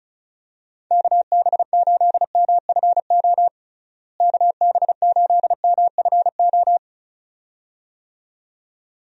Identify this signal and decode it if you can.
Morse code, heard as a keyed tone, about 36 wpm, 700 Hz. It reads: KB8MFO KB8MFO
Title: USA Call Signs - 6 Characters - Repeated 2x - 35wpm